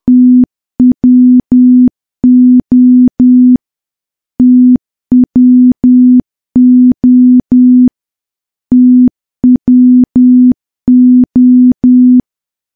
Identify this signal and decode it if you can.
Morse code, heard as a keyed tone, about 10 wpm, 250 Hz. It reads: TWO TWO TWO
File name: morse / morse-training-10WPM